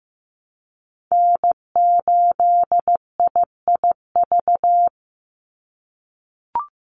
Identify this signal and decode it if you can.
Morse code, heard as a keyed tone, about 15 wpm, 700 Hz. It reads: N8IIV
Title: USA Call Signs 15wpm